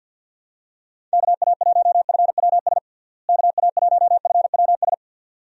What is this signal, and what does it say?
Morse code, heard as a keyed tone, about 52 wpm, 700 Hz. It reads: KA1FWS KA1FWS
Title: USA Call Signs - 6 Characters - Repeated 2x - 50wpm